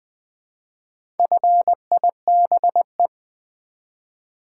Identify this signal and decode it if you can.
Morse code, heard as a keyed tone, about 20 wpm, 700 Hz. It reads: FIBE
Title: Build A Word - 1 Letter At A Time - 5-Letter Words - 20wpm